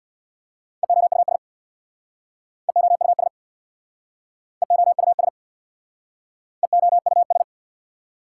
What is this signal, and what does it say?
Morse code, heard as a keyed tone, about 67 wpm, 690 Hz. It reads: EOUS EOUS EOUS EOUS